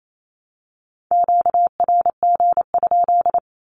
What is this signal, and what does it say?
Morse code, heard as a keyed tone, about 28 wpm, 700 Hz. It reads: QRG?
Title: Q-Codes 28wpm